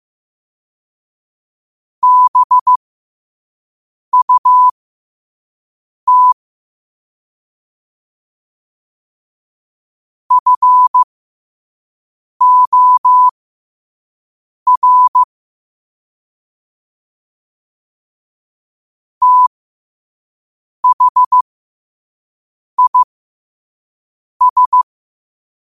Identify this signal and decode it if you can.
Morse code, heard as a keyed tone, about 15 wpm, spaced as for 5 wpm, 1000 Hz. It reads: BUT FOR THIS
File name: qotdfiles / 5wpm